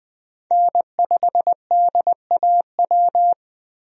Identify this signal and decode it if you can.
Morse code, heard as a keyed tone, about 20 wpm, 700 Hz. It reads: N5DAW
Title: USA Call Signs - 5 Characters - 20wpm